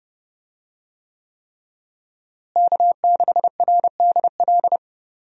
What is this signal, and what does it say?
Morse code, heard as a keyed tone, about 30 wpm, 700 Hz. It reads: K6RDL
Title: USA Call Signs - 5 Characters - Repeated 3x - 30wpm